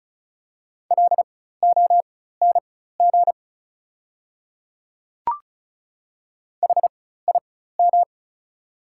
Morse code, heard as a keyed tone, about 36 wpm, spaced as for 16 wpm, 690 Hz. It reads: LONG HIM
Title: 4x Character Spacing - Top 100 Words - 35wpm